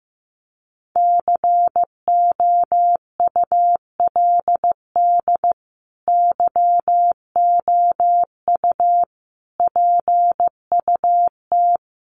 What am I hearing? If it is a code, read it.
Morse code, heard as a keyed tone, about 15 wpm, 700 Hz. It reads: COULD YOU PUT
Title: Sentences from Top 1400 Words 15wpm